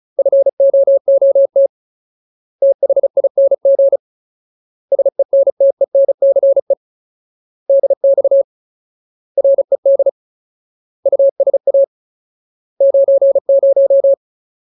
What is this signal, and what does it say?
Morse code, heard as a keyed tone, about 34 wpm, 550 Hz. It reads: FOOT THING SENTENCE DX RED USA 90
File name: Ultimate Word List / 35WPM Ultimate